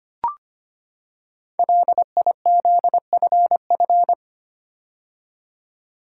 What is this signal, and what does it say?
Morse code, heard as a keyed tone, about 25 wpm, 700 Hz. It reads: LIZFF
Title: Code Groups - 5 Letters - 25wpm